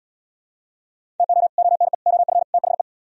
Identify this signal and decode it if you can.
Morse code, heard as a keyed tone, about 69 wpm, 690 Hz. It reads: TO ONE OF THE